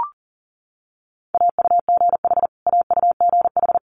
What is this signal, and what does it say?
Morse code, heard as a keyed tone, about 40 wpm, 700 Hz. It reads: AUGH AUGH
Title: Multi-letter Phonemes Repeated 3x 40wpm